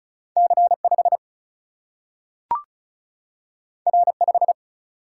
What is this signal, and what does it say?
Morse code, heard as a keyed tone, about 36 wpm, 700 Hz. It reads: C5 R5
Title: Single-Letter Single-Number 35wpm